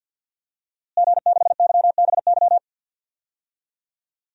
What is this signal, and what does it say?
Morse code, heard as a keyed tone, about 51 wpm, 700 Hz. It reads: G6YBY
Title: DXCC-100 Call Signs - 5 Characters - Repeated 2x - 50wpm